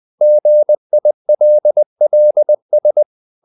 Morse code, heard as a keyed tone, about 20 wpm, 600 Hz. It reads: GILLS
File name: the old man and the sea part 2 20wpm